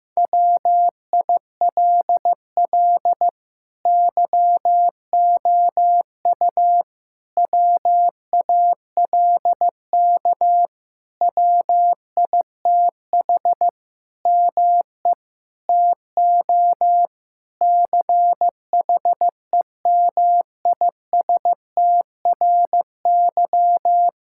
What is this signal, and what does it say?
Morse code, heard as a keyed tone, about 15 wpm, 700 Hz. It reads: WILL YOU WALK WITH ME TO CHEMISTRY
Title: Sentences from Top 1000 Words - Review Part 6 - 15wpm